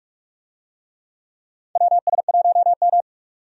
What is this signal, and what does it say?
Morse code, heard as a keyed tone, about 47 wpm, 700 Hz. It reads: WS1M
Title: USA Call Signs - 4 Characters - 45wpm